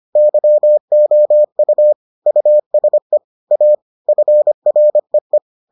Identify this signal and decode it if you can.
Morse code, heard as a keyed tone, about 25 wpm, 600 Hz. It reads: YOU USE A FREE